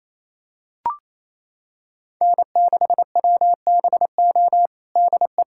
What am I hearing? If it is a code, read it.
Morse code, heard as a keyed tone, about 28 wpm, 700 Hz. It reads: N6WBO DE
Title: USA Callsign DE Callsign - 5 Characters - 28wpm